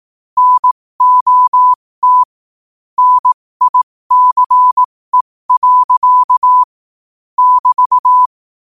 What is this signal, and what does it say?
Morse code, heard as a keyed tone, about 18 wpm, spaced as for 15 wpm, 1000 Hz. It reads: NOT NICE. <BT>
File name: qotdfiles / 15wpm